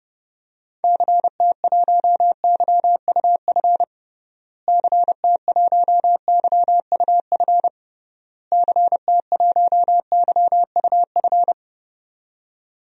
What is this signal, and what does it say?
Morse code, heard as a keyed tone, about 32 wpm, 700 Hz. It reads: CT1YUF CT1YUF CT1YUF
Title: DXCC-100 Call Signs - 6 Characters - Repeated 3x - 30wpm